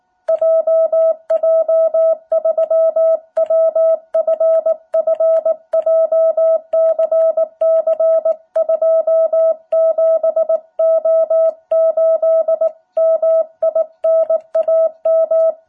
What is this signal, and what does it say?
Morse code, heard as a keyed tone, about 19 wpm, 660 Hz. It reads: JJ3WFF JCC27O8 MI NAM
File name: jh8cbh / 2021cw